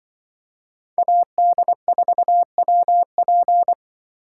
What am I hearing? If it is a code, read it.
Morse code, heard as a keyed tone, about 25 wpm, 700 Hz. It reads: AD4WP
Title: Speed Racing - USA 2x2 Call Signs 20wpm